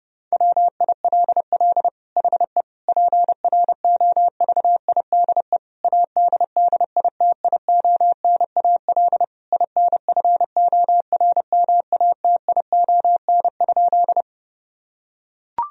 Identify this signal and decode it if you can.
Morse code, heard as a keyed tone, about 30 wpm, 700 Hz. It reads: WILL HE PROVIDE ADDITIONAL INFORMATION?
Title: Sentences from Top 1000 Words - Review Part 4 - 30wpm